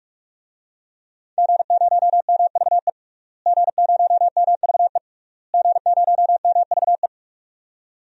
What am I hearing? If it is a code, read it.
Morse code, heard as a keyed tone, about 42 wpm, 700 Hz. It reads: G0MVE G0MVE G0MVE
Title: DXCC-100 Call Signs - 5 Characters - Repeated 3x - 45wpm